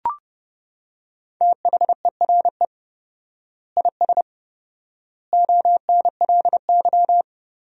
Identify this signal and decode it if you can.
Morse code, heard as a keyed tone, about 30 wpm, 700 Hz. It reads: THERE IS ONLY
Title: 4x Word-Spacing - Sentences from Top 300 Words - 30wpm